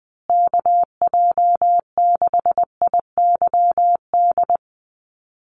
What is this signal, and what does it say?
Morse code, heard as a keyed tone, about 20 wpm, 700 Hz. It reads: KJ6IYD
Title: USA Call Signs Repeated 2x 20wpm